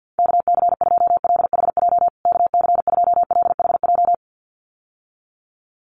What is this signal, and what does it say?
Morse code, heard as a keyed tone, about 50 wpm, 700 Hz. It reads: KC2LHJ KC2LHJ
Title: USA Call Signs Repeated 2x 50wpm